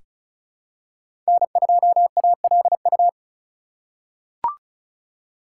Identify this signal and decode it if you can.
Morse code, heard as a keyed tone, about 36 wpm, 700 Hz. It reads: N2ALU
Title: USA Call Signs - 5 Characters - 2x Character Spacing - 35wpm